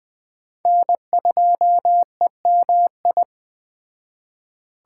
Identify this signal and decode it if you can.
Morse code, heard as a keyed tone, about 21 wpm, 700 Hz. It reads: N2EMI
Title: Speed Racing - USA Call Signs 17wpm